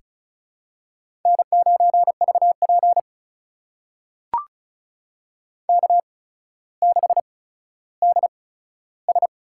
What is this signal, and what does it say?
Morse code, heard as a keyed tone, about 36 wpm, spaced as for 15 wpm, 700 Hz. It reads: N9VP K6DS
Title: USA Call Signs - 4 Characters - 8x Character Spacing - 35wpm